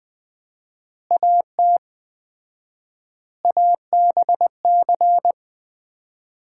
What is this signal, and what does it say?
Morse code, heard as a keyed tone, about 20 wpm, 700 Hz. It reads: AT ABC
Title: 4x Word-Spacing - Sets of 5 Words - 20wpm